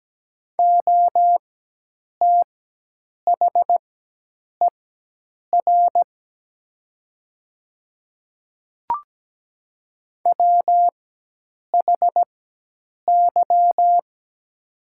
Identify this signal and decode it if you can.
Morse code, heard as a keyed tone, about 17 wpm, spaced as for 8 wpm, 700 Hz. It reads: OTHER WHY
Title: 4x Character Spacing - Top 200 Words - 17wpm